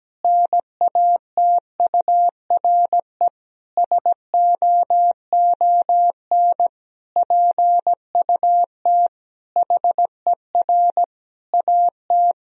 Morse code, heard as a keyed tone, about 17 wpm, 700 Hz. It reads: NATURE SOON PUT HER AT
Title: Sentences from Top 1000 Words - Review Part 1 - 17wpm